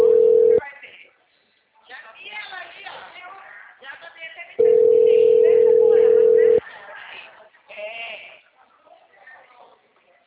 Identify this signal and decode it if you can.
Ringback tone (North America), 440 + 480 Hz, with speech.
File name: out- -2012- - - .8